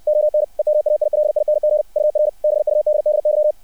Morse code, heard as a keyed tone, about 60 wpm, 600 Hz. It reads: ON EMAILING UR BUDDY
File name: archived / CRSnet